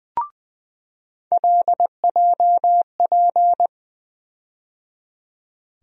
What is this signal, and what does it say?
Morse code, heard as a keyed tone, about 20 wpm, 700 Hz. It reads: LJP
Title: USA Call Sign Postfixes - 3 Letters 20wpm